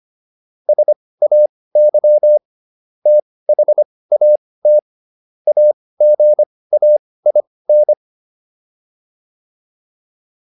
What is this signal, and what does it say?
Morse code, heard as a keyed tone, about 25 wpm, spaced as for 18 wpm, 600 Hz. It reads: SAY THAT AGAIN